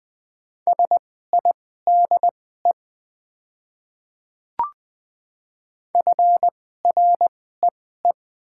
Morse code, heard as a keyed tone, about 21 wpm, spaced as for 15 wpm, 700 Hz. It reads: SIDE FREE
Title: 2x Character Spacing - Top 500 Words - 20wpm